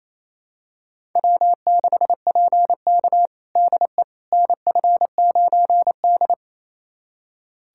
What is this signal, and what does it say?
Morse code, heard as a keyed tone, about 28 wpm, 700 Hz. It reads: W6PK DE NF9D
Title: USA Callsign DE Callsign - 4 Characters - 28wpm